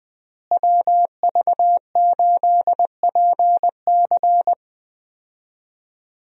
Morse code, heard as a keyed tone, about 20 wpm, 700 Hz. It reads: WV8PC
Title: USA Call Signs - 5 Characters - Repeated 3x - 20wpm